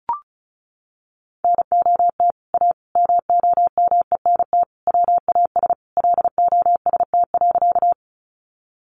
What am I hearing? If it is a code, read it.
Morse code, heard as a keyed tone, about 34 wpm, 700 Hz. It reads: NOT A MOMENT WAS LOST.
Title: Sentences from Top 1400 Words 35wpm